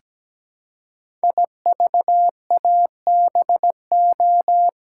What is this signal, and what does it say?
Morse code, heard as a keyed tone, about 17 wpm, 700 Hz. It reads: IVABO
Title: Code Groups - 5 Letters - 17wpm